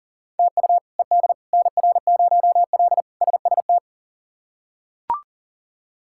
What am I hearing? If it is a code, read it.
Morse code, heard as a keyed tone, about 40 wpm, 700 Hz. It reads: TU ED NR0L SST
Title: K1USN SST QSO Practice - 40wpm